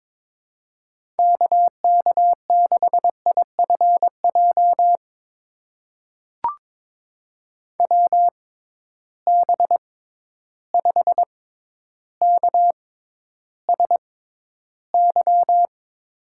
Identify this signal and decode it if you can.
Morse code, heard as a keyed tone, about 22 wpm, spaced as for 11 wpm, 700 Hz. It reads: KK6IFJ WB5KSY
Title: USA Call Signs - 6 Characters - 6x Character Spacing - 22wpm